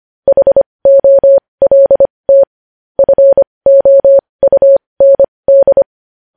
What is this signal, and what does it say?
Morse code, heard as a keyed tone, about 25 wpm, spaced as for 20 wpm, 550 Hz. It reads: HOLT FOUND